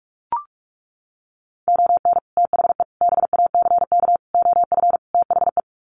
Morse code, heard as a keyed tone, about 44 wpm, 700 Hz. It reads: ON THE BACK OF THE
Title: Sets of 5 Words 45wpm